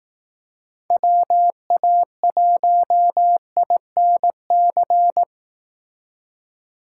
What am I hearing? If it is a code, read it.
Morse code, heard as a keyed tone, about 18 wpm, 700 Hz. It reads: WA1INC
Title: Speed Racing - USA 2x3 Call Signs 15wpm